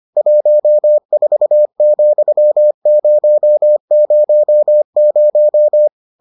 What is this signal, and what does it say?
Morse code, heard as a keyed tone, about 25 wpm, 600 Hz. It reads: 14,000